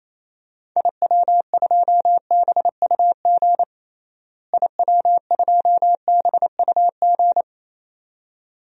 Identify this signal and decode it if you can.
Morse code, heard as a keyed tone, about 27 wpm, 700 Hz. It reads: IW2BUG IW2BUG
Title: DXCC-100 Call Signs - 6 Characters - Repeated 2x - 28wpm